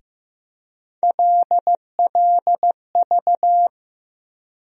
Morse code, heard as a keyed tone, about 15 wpm, 700 Hz. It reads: LLV
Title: USA Call Sign Postfixes - 3 Letters 15wpm